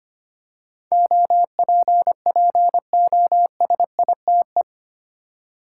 Morse code, heard as a keyed tone, about 25 wpm, 700 Hz. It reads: OPPOSITE
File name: Eight Letter Words - 25wpm